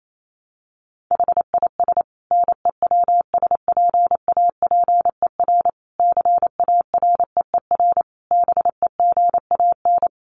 Morse code, heard as a keyed tone, about 28 wpm, 700 Hz. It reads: HIS NEWSPAPER CAREER BEGAN
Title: Sentences from Top 1100 Words 28wpm